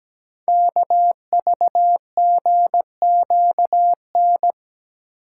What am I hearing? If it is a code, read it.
Morse code, heard as a keyed tone, about 17 wpm, 700 Hz. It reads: KVGQN
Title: Code Groups - 5 Letters - 17wpm